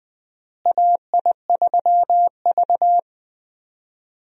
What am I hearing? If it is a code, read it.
Morse code, heard as a keyed tone, about 20 wpm, 700 Hz. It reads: AI3V
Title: USA Call Signs - 4 Characters - Repeated 2x - 20wpm